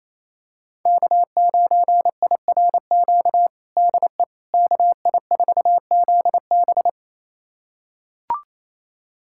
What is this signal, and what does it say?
Morse code, heard as a keyed tone, about 28 wpm, 690 Hz. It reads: K9IRQ DE KI4ZB
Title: USA Callsign DE Callsign - 5 Characters - 28wpm